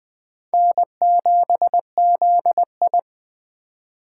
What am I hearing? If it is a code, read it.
Morse code, heard as a keyed tone, about 19 wpm, 700 Hz. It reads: N7ZI